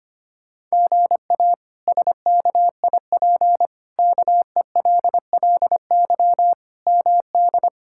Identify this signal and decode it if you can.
Morse code, heard as a keyed tone, about 25 wpm, 700 Hz. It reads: GA SKIP KELLY MD
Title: K1USN SST QSO Practice - 25wpm